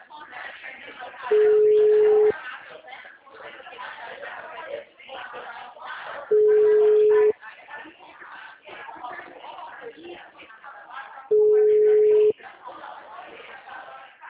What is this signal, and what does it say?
Ringback tone (Europe, 425 Hz), with speech.
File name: out- -2010- - - .10275